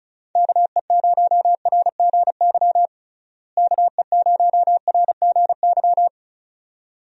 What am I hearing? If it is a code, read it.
Morse code, heard as a keyed tone, about 36 wpm, 700 Hz. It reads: KE0RGY KE0RGY
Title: USA Call Signs - 6 Characters - Repeated 2x - 35wpm